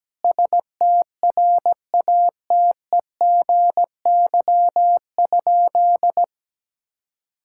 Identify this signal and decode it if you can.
Morse code, heard as a keyed tone, about 17 wpm, 700 Hz. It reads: STRATEGY?
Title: Sentences from Top 1000 Words - Review Part 5 - 17wpm